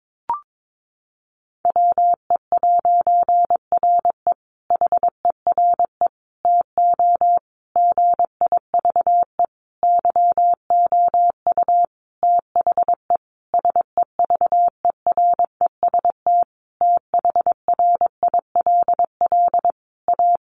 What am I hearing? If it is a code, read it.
Morse code, heard as a keyed tone, about 22 wpm, 700 Hz. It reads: WE'RE HERE TO GIVE YOU THE SEVEREST THRILL A
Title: A Sound Of Thunder 22wpm